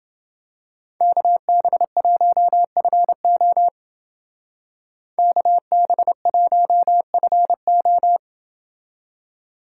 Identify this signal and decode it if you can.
Morse code, heard as a keyed tone, about 28 wpm, 700 Hz. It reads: KB1FO KB1FO